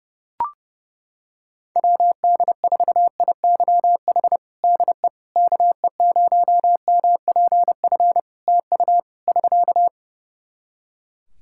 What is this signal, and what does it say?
Morse code, heard as a keyed tone, about 32 wpm, 700 Hz. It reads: WD4IYH DE KE0MPF TU <SK>
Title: Practice QSO - Exchange 4 - Ending - 30wpm